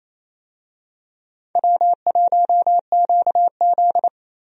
Morse code, heard as a keyed tone, about 28 wpm, 700 Hz. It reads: W1QZ
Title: USA Call Signs - 4 Characters - Repeated 2x - 28wpm